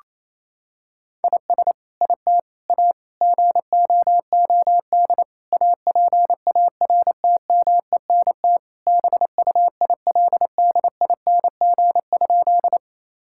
Morse code, heard as a keyed tone, about 28 wpm, 700 Hz. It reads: IS IT A GOOD APARTMENT BUILDING?
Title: Sentences from Top 1000 Words - Review Part 6 - 28wpm